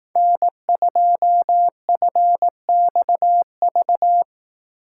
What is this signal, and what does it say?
Morse code, heard as a keyed tone, about 18 wpm, 700 Hz. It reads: N2FXV
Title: Speed Racing - USA 1x3 Call Signs 15wpm